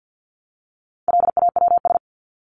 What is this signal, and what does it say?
Morse code, heard as a keyed tone, about 50 wpm, 700 Hz. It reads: LAWS
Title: ES:S Suffix - Top 100 Words with Base Words 50wpm